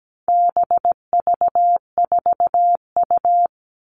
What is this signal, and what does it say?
Morse code, heard as a keyed tone, about 17 wpm, 700 Hz. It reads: BV4U
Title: All Cut Numbers - 4-Digit 17wpm